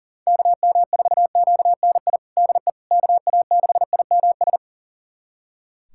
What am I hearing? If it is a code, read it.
Morse code, heard as a keyed tone, about 40 wpm, 700 Hz. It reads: KM4QNI DE KA6IMS
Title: USA Callsign DE Callsign - 6 Characters - 40wpm